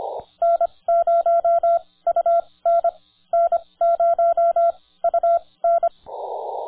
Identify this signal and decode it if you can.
Morse code, heard as a keyed tone, about 26 wpm, spaced as for 19 wpm, 1360 Hz, with radio noise. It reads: N0UN N0UN
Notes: Viet Nam on 30 Meters CW at 9:00 AM our local time